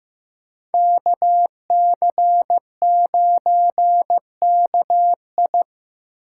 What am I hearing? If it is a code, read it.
Morse code, heard as a keyed tone, about 18 wpm, 700 Hz. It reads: KC9KI
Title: Speed Racing - USA 2x2 Call Signs 15wpm